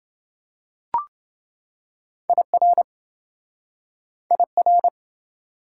Morse code, heard as a keyed tone, about 28 wpm, 700 Hz. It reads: IR IR